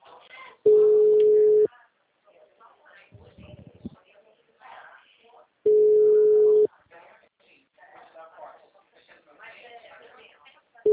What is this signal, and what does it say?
Ringback tone (Europe, 425 Hz), with speech.
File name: out- -2035- - - .54211